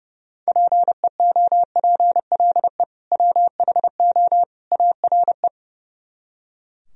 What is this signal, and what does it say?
Morse code, heard as a keyed tone, about 35 wpm, 700 Hz. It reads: PEOPLE WHO ARE
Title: Speed-Racing - Sets of 3 Words 30wpm